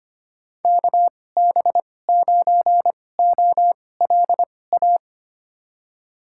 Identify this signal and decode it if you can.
Morse code, heard as a keyed tone, about 25 wpm, spaced as for 18 wpm, 700 Hz. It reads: KB9OLA
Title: USA Call Signs - 6 Characters - 2x Character Spacing - 25wpm